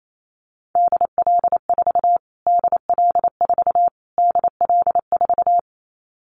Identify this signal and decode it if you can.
Morse code, heard as a keyed tone, about 28 wpm, 700 Hz. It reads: DL4 DL4 DL4
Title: Callsign Prefixes for Top 25 DXCC - Repeated 3x - 28wpm